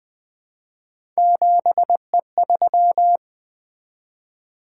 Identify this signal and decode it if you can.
Morse code, heard as a keyed tone, about 20 wpm, 700 Hz. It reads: 7E3
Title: Three Letter-Number Combinations - 20wpm